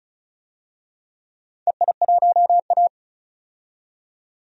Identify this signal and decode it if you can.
Morse code, heard as a keyed tone, about 36 wpm, 700 Hz. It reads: EI1A
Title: DXCC-100 Call Signs - 4 Characters - 35wpm